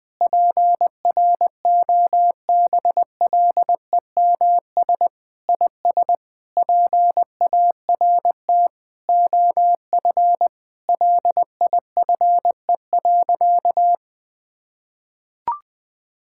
Morse code, heard as a keyed tone, about 20 wpm, 700 Hz. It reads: PROBLEMS IS PART OF LIFE.
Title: Sentences with 3-Word Phrases - Spoken Beginning - 20wpm